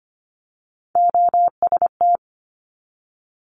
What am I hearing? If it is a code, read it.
Morse code, heard as a keyed tone, about 25 wpm, 700 Hz. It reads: OST
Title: Multi-letter Phonemes 25wpm